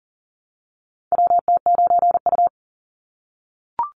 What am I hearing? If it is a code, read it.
Morse code, heard as a keyed tone, about 40 wpm, 700 Hz. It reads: WT9U
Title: Call Signs - 40wpm